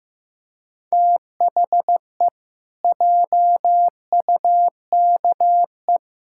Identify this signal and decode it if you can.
Morse code, heard as a keyed tone, about 15 wpm, 700 Hz. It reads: THE JUKE
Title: Pangram Challenge v2 - 15wpm